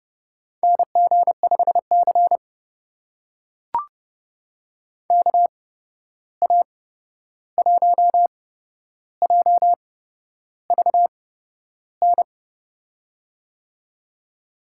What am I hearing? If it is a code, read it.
Morse code, heard as a keyed tone, about 32 wpm, spaced as for 11 wpm, 700 Hz. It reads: NG5C KA1JVN
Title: USA Call Signs - 8x Character Spacing - 30wpm